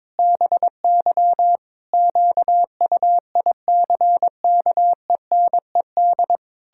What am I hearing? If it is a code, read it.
Morse code, heard as a keyed tone, about 22 wpm, 700 Hz. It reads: BY QUICKENED
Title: Pangram Challenge v2 - 22wpm